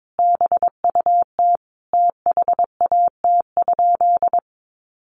Morse code, heard as a keyed tone, about 22 wpm, 700 Hz. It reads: BUT THAT?
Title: Sentences from Top 1400 Words 22wpm